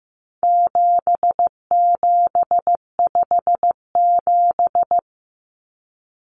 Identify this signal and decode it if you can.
Morse code, heard as a keyed tone, about 15 wpm, 700 Hz. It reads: 7757
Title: Numbers - 4-Digit 15wpm